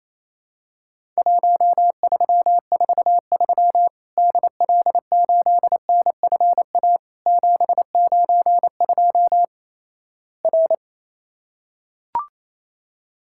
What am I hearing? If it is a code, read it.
Morse code, heard as a keyed tone, about 28 wpm, 700 Hz. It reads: 1343 DL8NFA 792 R
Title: Worked All Europe DX Contest - QTC Exchange - 28wpm